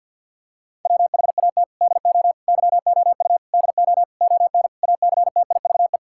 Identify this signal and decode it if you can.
Morse code, heard as a keyed tone, about 50 wpm, 700 Hz. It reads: WHAT DO YOU DO ON ACTIVE
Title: Sentences from Top 1000 Words - Review Part 6 - 50wpm